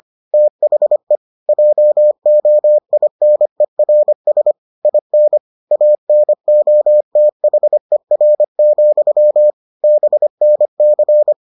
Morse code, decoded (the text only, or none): THE JOINERS IN ANOTHER, BNC